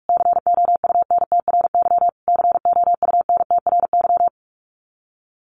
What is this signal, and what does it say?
Morse code, heard as a keyed tone, about 46 wpm, 700 Hz. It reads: COUNTRY COUNTRY
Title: Top 300 Words Repeated 45wpm